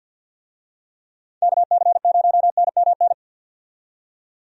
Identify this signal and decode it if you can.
Morse code, heard as a keyed tone, about 53 wpm, 700 Hz. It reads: KC0NMN
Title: USA Call Signs - 6 Characters - 50wpm